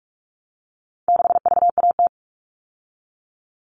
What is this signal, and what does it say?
Morse code, heard as a keyed tone, about 44 wpm, 700 Hz. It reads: 6VAT
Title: All Cut Numbers - 4-Digit 45wpm